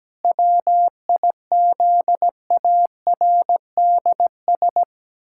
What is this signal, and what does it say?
Morse code, heard as a keyed tone, about 17 wpm, 700 Hz. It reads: WIZARDS
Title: Pangram Challenge v2 - 17wpm